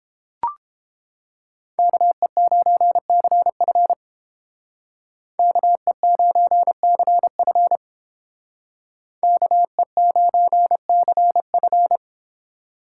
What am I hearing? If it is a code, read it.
Morse code, heard as a keyed tone, about 29 wpm, 700 Hz. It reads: KE9CF KE9CF KE9CF